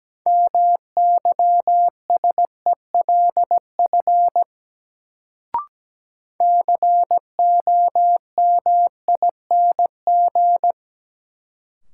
Morse code, heard as a keyed tone, about 17 wpm, 700 Hz. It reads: MYSELF COMING
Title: Sentences from Top 1300 Words - New Word Drill - 17wpm